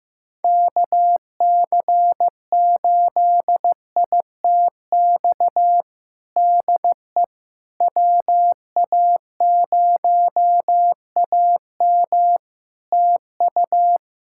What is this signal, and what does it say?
Morse code, heard as a keyed tone, about 16 wpm, 700 Hz. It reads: KC8ITX DE WA0AM TU
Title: Practice QSO - Exchange 4 - Ending - 15wpm